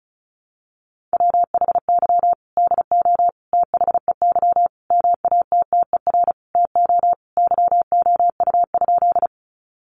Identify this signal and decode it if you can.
Morse code, heard as a keyed tone, about 34 wpm, 700 Hz. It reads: WHY DO THEY MATTER TO YOU?
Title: Sentences from Top 700 Words 35wpm